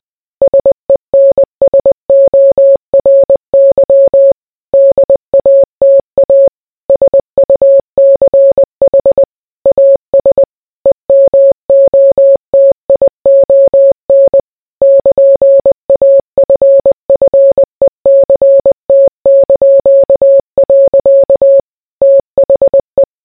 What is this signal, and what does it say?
Morse code, heard as a keyed tone, about 20 wpm, 550 Hz. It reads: SENSORY DATA SUCH AS EMOTION <KN>AFFECT). THE